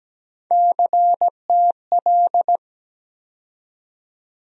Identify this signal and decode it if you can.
Morse code, heard as a keyed tone, about 17 wpm, 700 Hz. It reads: CTL
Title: USA Call Sign Postfixes - 3 Letters 17wpm